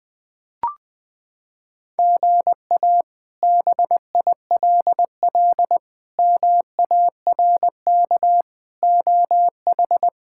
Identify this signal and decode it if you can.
Morse code, heard as a keyed tone, about 20 wpm, 700 Hz. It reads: GA BILL MARK OH
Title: K1USN SST QSO Practice - 20wpm